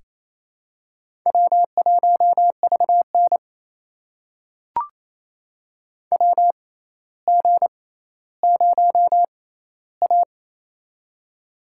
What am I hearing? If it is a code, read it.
Morse code, heard as a keyed tone, about 28 wpm, spaced as for 14 wpm, 700 Hz. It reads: W1VN WG0A
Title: USA Call Signs - 4 Characters - 6x Character Spacing - 28wpm